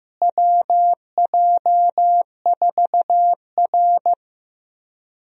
Morse code, heard as a keyed tone, about 15 wpm, 700 Hz. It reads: WJ4R
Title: USA Call Signs - 4 Characters - Repeated 3x - 15wpm